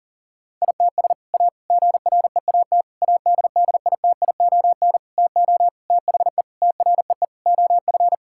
Morse code, heard as a keyed tone, about 40 wpm, 700 Hz. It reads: ITS A GREAT ADDITION TO THE TREE OF
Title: Sentences from Top 1000 Words - Review Part 6 - 40wpm